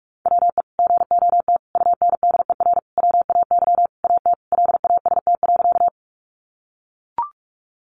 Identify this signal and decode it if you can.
Morse code, heard as a keyed tone, about 44 wpm, 700 Hz. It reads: WE GOT UNDER WAY AT LAST.
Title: Sentences from Top 300 Words 45wpm